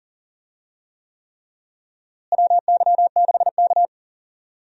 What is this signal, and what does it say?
Morse code, heard as a keyed tone, about 40 wpm, 700 Hz. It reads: WY6K
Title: USA Call Signs - 4 Characters - Repeated 3x - 40wpm